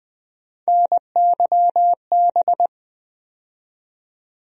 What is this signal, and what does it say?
Morse code, heard as a keyed tone, about 20 wpm, 700 Hz. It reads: NYB
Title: USA Call Sign Postfixes - 3 Letters 20wpm